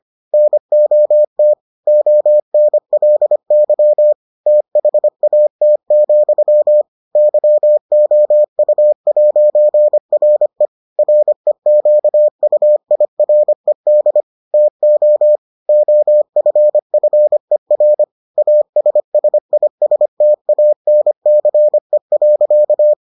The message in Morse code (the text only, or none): NOT ONLY THAT, YOU'RE REQUIRED TO OFFER ASSISTANCE.